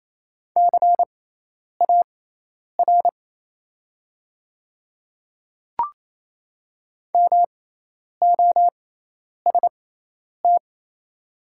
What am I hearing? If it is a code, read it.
Morse code, heard as a keyed tone, about 28 wpm, spaced as for 10 wpm, 700 Hz. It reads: CAR MOST
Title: 6x Character Spacing - Top 500 Words - 28wpm